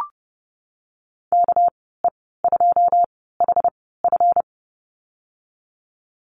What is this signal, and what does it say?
Morse code, heard as a keyed tone, about 32 wpm, spaced as for 17 wpm, 700 Hz. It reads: KE2HF
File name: Lesson 37g - Letter x - Call Signs 30wpm